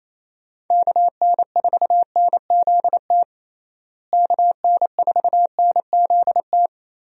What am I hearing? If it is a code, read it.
Morse code, heard as a keyed tone, about 27 wpm, 700 Hz. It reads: KN4NZT KN4NZT
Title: USA Call Signs - 6 Characters - Repeated 3x - 28wpm